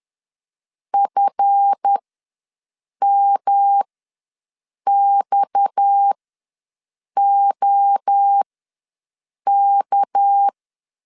Morse code, heard as a keyed tone, about 10 wpm, spaced as for 6 wpm, 800 Hz. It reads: FMXOK
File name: LEZIONI / 7 GF9Z / 7 GF9Z ws 10wpm 3919